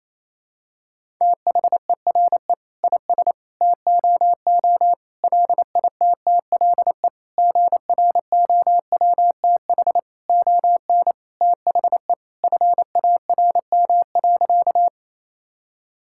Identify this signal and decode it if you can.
Morse code, heard as a keyed tone, about 27 wpm, 700 Hz. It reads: THERE IS TOO LITTLE GROWTH ON THE FARM.
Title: Sentences from Top 1000 Words - Review Part 5 - 28wpm